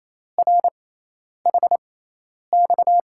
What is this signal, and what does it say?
Morse code, heard as a keyed tone, about 27 wpm, spaced as for 10 wpm, 700 Hz. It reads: RHX
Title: USA Call Signs - 5 Characters - 6x Character Spacing - 28wpm